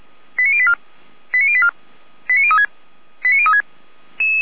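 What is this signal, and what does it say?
Five-tone selective call (ZVEI), heard as quick runs of five tones, tones of about 70 ms. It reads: 89084 89084 89036 89036
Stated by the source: ZVEI five-tone call received on an SDR.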